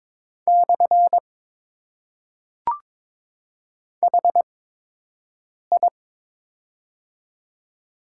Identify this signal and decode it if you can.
Morse code, heard as a keyed tone, about 21 wpm, spaced as for 6 wpm, 700 Hz. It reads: / HI
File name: Top 90 QSO Elements - 8x Character Spacing - 22wpm